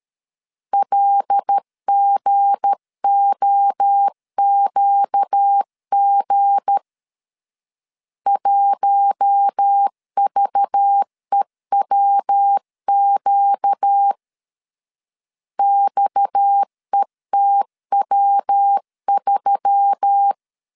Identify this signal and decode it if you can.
Morse code, heard as a keyed tone, about 13 wpm, 800 Hz. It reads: LGOQG 1VEWQ XETW3
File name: LEZIONI / 7 GF9Z / 7 G 12wpm 2055